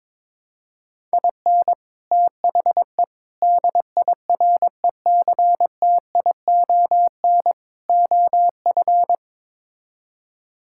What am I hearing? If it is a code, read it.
Morse code, heard as a keyed tone, about 22 wpm, 700 Hz. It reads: IN THE DIRECTION OF
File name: practice / 22wpm / Build A Phrase - 5-Word Phrases - 22wpm